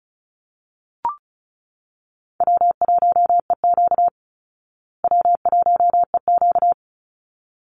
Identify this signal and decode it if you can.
Morse code, heard as a keyed tone, about 34 wpm, 700 Hz. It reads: W1EQ W1EQ
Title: Easy Call Signs Repeated 2x - 35wpm